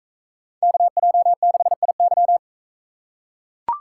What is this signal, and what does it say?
Morse code, heard as a keyed tone, about 40 wpm, 700 Hz. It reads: KJ6IY